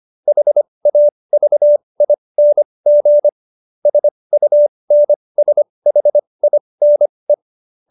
Morse code, heard as a keyed tone, about 25 wpm, spaced as for 20 wpm, 590 Hz. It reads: HAVING SUNSHINE